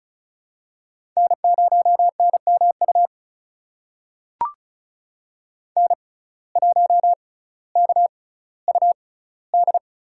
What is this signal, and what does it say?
Morse code, heard as a keyed tone, about 34 wpm, spaced as for 18 wpm, 700 Hz. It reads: N0NMU N1KUD
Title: USA Call Signs - 5 Characters - 6x Character Spacing - 35wpm